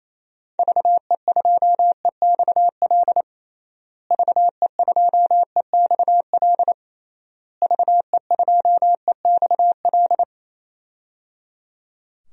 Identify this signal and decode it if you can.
Morse code, heard as a keyed tone, about 28 wpm, 700 Hz. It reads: VE2EXL VE2EXL VE2EXL
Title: DXCC-100 Call Signs - 6 Characters - Repeated 3x - 28wpm